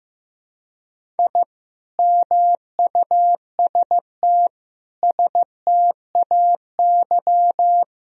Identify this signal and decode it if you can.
Morse code, heard as a keyed tone, about 15 wpm, 700 Hz. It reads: I MUST STAY
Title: Sentences from Top 1000 Words - Review Part 1 - 15wpm